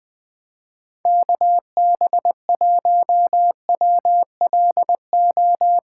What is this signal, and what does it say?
Morse code, heard as a keyed tone, about 21 wpm, 700 Hz. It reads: KB1WLO
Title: USA Callsign DE Callsign 20wpm